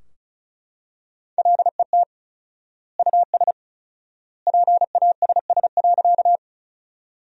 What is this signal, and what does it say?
Morse code, heard as a keyed tone, about 36 wpm, 700 Hz. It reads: LET US PASS.
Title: 4x Word-Spacing - Sentences from Top 400 Words - 35wpm